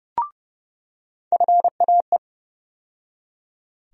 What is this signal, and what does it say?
Morse code, heard as a keyed tone, about 30 wpm, 700 Hz. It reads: FAE
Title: USA Call Sign Postfixes - 3 Letters 30wpm